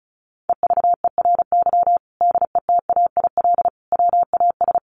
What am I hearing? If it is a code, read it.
Morse code, heard as a keyed tone, about 34 wpm, 700 Hz. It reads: EVERY DETAIL WAS
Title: Sentences from Top 1500 Words 35wpm